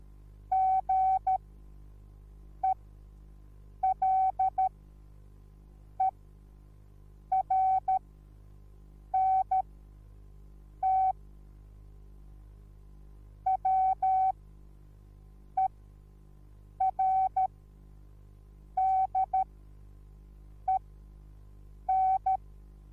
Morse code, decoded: GELERNT WERDEN